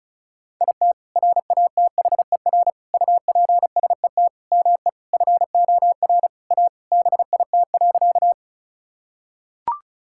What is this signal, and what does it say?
Morse code, heard as a keyed tone, about 36 wpm, 690 Hz. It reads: IT RATHER UPSET ME FOR A BIT.
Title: Sentences from Top 1000 Words - Review Part 5 - 35wpm